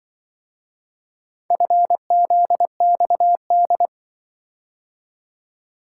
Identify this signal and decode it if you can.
Morse code, heard as a keyed tone, about 26 wpm, 700 Hz. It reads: FZXD